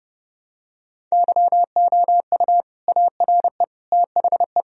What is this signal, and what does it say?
Morse code, heard as a keyed tone, about 30 wpm, 700 Hz. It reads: YOU ARE THE
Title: Sentences from Top 1000 Words - Review Part 2 - 30wpm